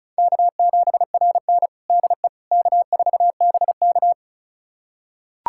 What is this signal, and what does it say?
Morse code, heard as a keyed tone, about 36 wpm, 700 Hz. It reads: K7RN DE K4BK
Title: USA Callsign DE Callsign - 4 Characters - 35wpm